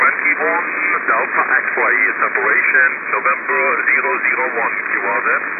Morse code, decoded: none